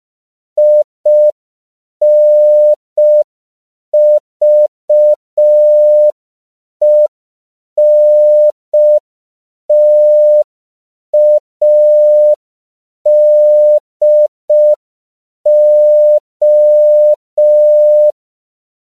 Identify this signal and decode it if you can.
Morse code, heard as a keyed tone, about 5 wpm, 600 Hz. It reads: INVENTADO